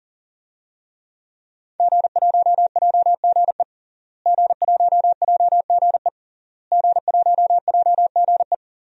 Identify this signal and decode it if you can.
Morse code, heard as a keyed tone, about 40 wpm, 700 Hz. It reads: G1JGE G1JGE G1JGE
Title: DXCC-100 Call Signs - 5 Characters - Repeated 3x - 40wpm